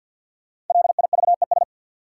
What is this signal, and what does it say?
Morse code, heard as a keyed tone, about 53 wpm, 700 Hz. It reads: LIVES
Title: Sets of 5 Words Encore - New Word Drill - 50wpm